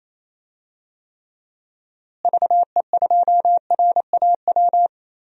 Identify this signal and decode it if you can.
Morse code, heard as a keyed tone, about 28 wpm, 700 Hz. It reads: VE2RAW
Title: DXCC-100 Call Signs - 6 Characters - Repeated 3x - 28wpm